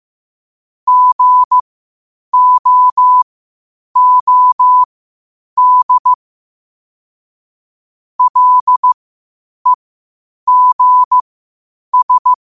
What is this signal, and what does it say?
Morse code, heard as a keyed tone, about 15 wpm, spaced as for 8 wpm, 1000 Hz. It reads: GOOD LEGS
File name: qotdfiles / 8wpm